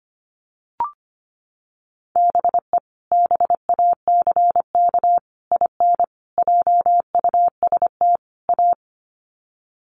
Morse code, heard as a keyed tone, about 25 wpm, 700 Hz. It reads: BE BACK IN JUST A
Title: Sets of 5 Words Encore 25wpm